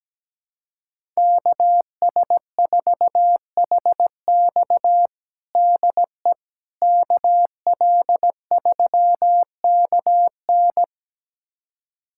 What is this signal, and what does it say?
Morse code, heard as a keyed tone, about 17 wpm, 700 Hz. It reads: KS4HX DE KL3KN
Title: USA Callsign DE Callsign - 5 Characters - 17wpm